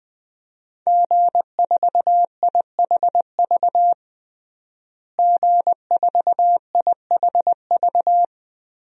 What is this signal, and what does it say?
Morse code, heard as a keyed tone, about 21 wpm, 700 Hz. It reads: G4IHV G4IHV
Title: DXCC-100 Call Signs - 5 Characters - Repeated 2x - 20wpm